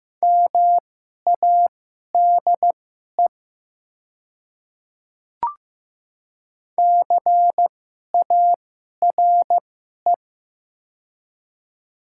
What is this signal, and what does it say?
Morse code, heard as a keyed tone, about 15 wpm, spaced as for 11 wpm, 690 Hz. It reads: MADE CARE
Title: 2x Character Spacing - Top 300 Words - 15wpm